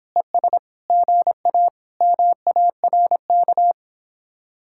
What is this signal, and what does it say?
Morse code, heard as a keyed tone, about 26 wpm, 700 Hz. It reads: ES GA MARK